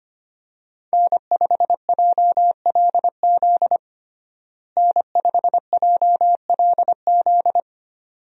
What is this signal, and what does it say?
Morse code, heard as a keyed tone, about 25 wpm, 700 Hz. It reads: N5JLZ N5JLZ
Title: USA Call Signs - 5 Characters - Repeated 2x - 25wpm